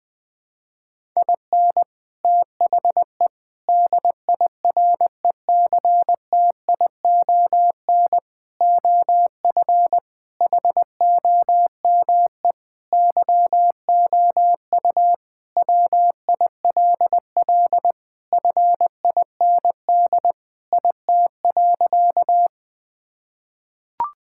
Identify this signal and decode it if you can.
Morse code, heard as a keyed tone, about 20 wpm, 700 Hz. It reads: IN THE DIRECTION OF HOME YOU WILL FIND IT.
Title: Sentences with 4-Word Phrases - 20wpm